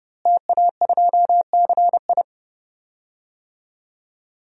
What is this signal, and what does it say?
Morse code, heard as a keyed tone, about 31 wpm, 700 Hz. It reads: TA2CI
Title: DXCC-100 Call Signs - 5 Characters - Repeated 2x - 30wpm